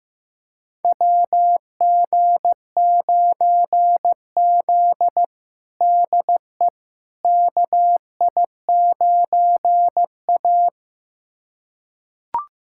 Morse code, heard as a keyed tone, about 15 wpm, 700 Hz. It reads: WG9Z DE KI9A
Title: USA Callsign DE Callsign - 4 Characters - 15wpm